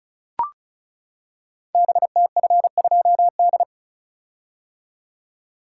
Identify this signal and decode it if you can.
Morse code, heard as a keyed tone, about 36 wpm, 690 Hz. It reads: BTF2D
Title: Code Groups - 5 Letter:Numbers - 35wpm